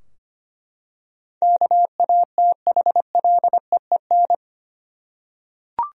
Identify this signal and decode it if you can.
Morse code, heard as a keyed tone, about 25 wpm, 700 Hz. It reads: KATHLEEN
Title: USA Common Names - 25wpm